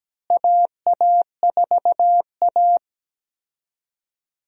Morse code, heard as a keyed tone, about 17 wpm, 700 Hz. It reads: AA4A
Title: USA Call Signs - 4 Characters - Repeated 2x - 17wpm